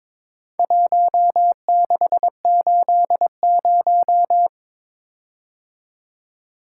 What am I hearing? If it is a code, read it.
Morse code, heard as a keyed tone, about 22 wpm, 700 Hz. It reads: 1680
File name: practice / 22wpm / Four Letter-Number Combinations - 22wpm